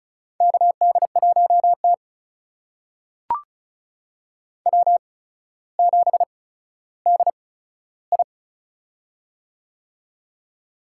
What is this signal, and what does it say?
Morse code, heard as a keyed tone, about 35 wpm, spaced as for 15 wpm, 700 Hz. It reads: KD1T W7DI
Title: USA Call Signs - 4 Characters - 8x Character Spacing - 35wpm